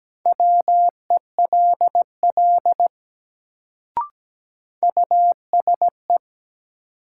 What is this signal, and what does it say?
Morse code, heard as a keyed tone, about 17 wpm, 700 Hz. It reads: WELL USE
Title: Sets of 2 Words - New Word Drill - 17wpm